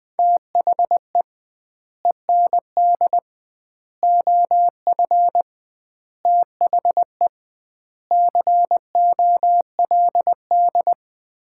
Morse code, heard as a keyed tone, about 20 wpm, 700 Hz. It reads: THE END OF THE COLD
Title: 2x Word-Spacing - Sets of 5 Words - 20wpm